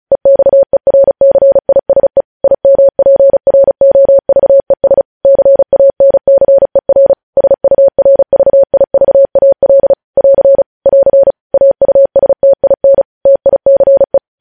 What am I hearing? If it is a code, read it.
Morse code, heard as a keyed tone, about 36 wpm, 550 Hz. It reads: EXERCISE IMPROVES CANCER SURVIVAL <AR> <AR> AUSTIN TICE
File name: podcast / News headlines-1x-35x35